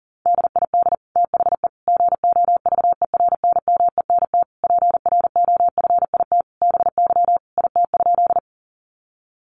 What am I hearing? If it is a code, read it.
Morse code, heard as a keyed tone, about 40 wpm, 690 Hz. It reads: DID THE GOVERNMENT PROFIT BY IT?
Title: Sentences from Top 800 Words 40wpm